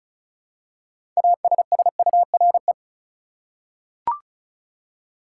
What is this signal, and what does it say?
Morse code, heard as a keyed tone, about 34 wpm, 700 Hz. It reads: ASSURE
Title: Six Letter Words - 35wpm